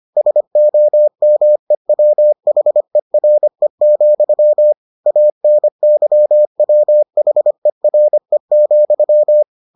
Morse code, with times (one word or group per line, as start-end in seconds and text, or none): 0.2-4.7 SOMEWHERE,
5.1-9.4 ANYWHERE,